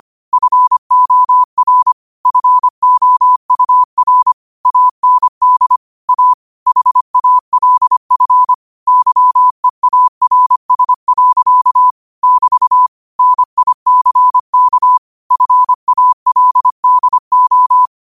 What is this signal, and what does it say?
Morse code, heard as a keyed tone, about 25 wpm, 1000 Hz. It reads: FOR FOUR AND A HALF YEARS. <BT> NICK FALDO
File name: qotdfiles / 25wpm